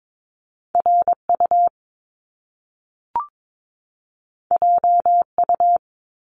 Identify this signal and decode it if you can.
Morse code, heard as a keyed tone, about 22 wpm, 700 Hz. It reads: RU JU
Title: Top Two Letter Combinations in Words 22wpm